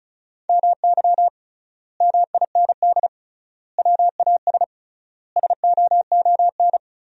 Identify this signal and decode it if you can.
Morse code, heard as a keyed tone, about 36 wpm, 700 Hz. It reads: MY MIND WAS SOON
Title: 3x Word-Spacing - Sentences from Top 500 Words - 35wpm